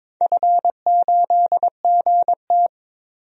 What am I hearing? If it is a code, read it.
Morse code, heard as a keyed tone, about 22 wpm, 700 Hz. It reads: F8GT
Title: DXCC-100 Call Signs - 4 Characters - 22wpm